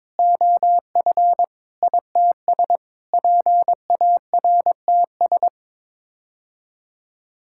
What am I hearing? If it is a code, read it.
Morse code, heard as a keyed tone, about 22 wpm, 700 Hz. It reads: OF ITS PARTS
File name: practice / 22wpm / Sets of 8 Words - 22wpm